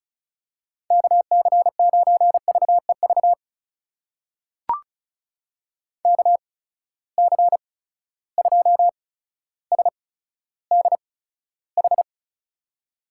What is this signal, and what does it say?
Morse code, heard as a keyed tone, about 36 wpm, spaced as for 15 wpm, 700 Hz. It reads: KC9VEV KC2SDH
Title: USA Call Signs - 6 Characters - 8x Character Spacing - 35wpm